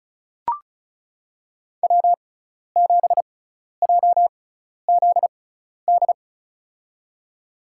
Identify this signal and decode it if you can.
Morse code, heard as a keyed tone, about 34 wpm, spaced as for 12 wpm, 700 Hz. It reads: W7JZD
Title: USA Call Signs - 5 Characters - 6x Character Spacing - 35wpm